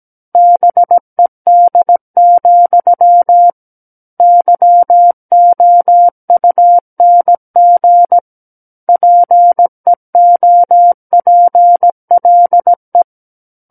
Morse code, heard as a keyed tone, about 17 wpm, 700 Hz. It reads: BED, YOUNG PEOPLE
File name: Book One / 08-Chapter 8 - 17 WPM